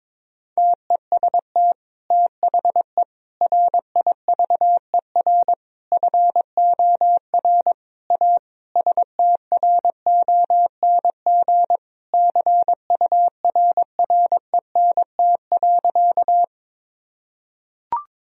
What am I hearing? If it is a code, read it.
Morse code, heard as a keyed tone, about 22 wpm, 700 Hz. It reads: TEST THE RIVER FOR A STRONG CURRENT.
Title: Sentences from Top 1000 Words - Review Part 4 - 22wpm